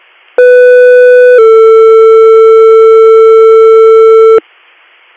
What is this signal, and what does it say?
Two-tone page, heard as a one-second tone, then a three-second tone, about 500 then 450 Hz, with noise. It is heard from 0.4 to 4.4 s.